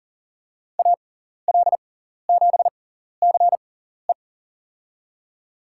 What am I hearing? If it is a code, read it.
Morse code, heard as a keyed tone, about 39 wpm, spaced as for 14 wpm, 700 Hz. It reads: AL7CE
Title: USA Call Signs - 6x Character Spacing - 40wpm